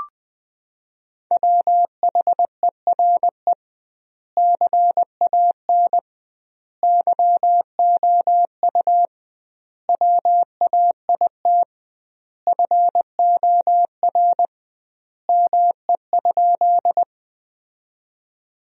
Morse code, heard as a keyed tone, about 20 wpm, 700 Hz. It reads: WHERE CAN YOU WAIT FOR ME?
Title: 2x Word-Spacing - Sentences from Top 500 Words - 20wpm